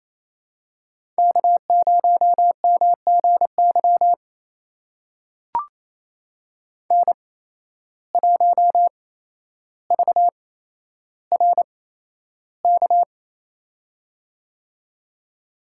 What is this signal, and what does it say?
Morse code, heard as a keyed tone, about 27 wpm, spaced as for 12 wpm, 700 Hz. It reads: K0MGY N1VRK
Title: USA Call Signs - 5 Characters - 8x Character Spacing - 28wpm